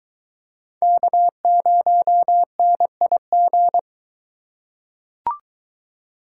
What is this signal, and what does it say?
Morse code, heard as a keyed tone, about 22 wpm, 700 Hz. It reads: K0NIG